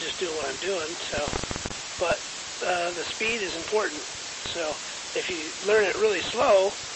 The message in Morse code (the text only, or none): none